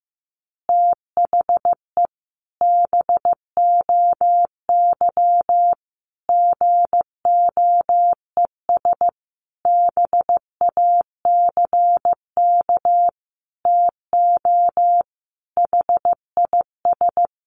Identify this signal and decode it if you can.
Morse code, heard as a keyed tone, about 15 wpm, 700 Hz. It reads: THE BOY GOES BACK TO HIS
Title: Additional Easy Phrases 15wpm